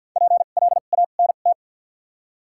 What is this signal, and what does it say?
Morse code, heard as a keyed tone, about 48 wpm, 700 Hz. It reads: P L A N T
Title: 2x Character Spacing - Top 300 Words - 50wpm